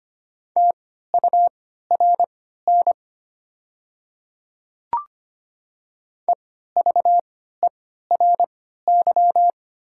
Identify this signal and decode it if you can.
Morse code, heard as a keyed tone, about 25 wpm, spaced as for 14 wpm, 700 Hz. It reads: TURN EVERY
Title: 3x Character Spacing - Top 200 Words - 25wpm